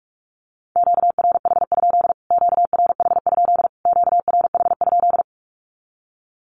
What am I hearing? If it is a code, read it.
Morse code, heard as a keyed tone, about 44 wpm, 700 Hz. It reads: QRH? QRH? QRH?
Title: Q-Codes Repeated 3x 45wpm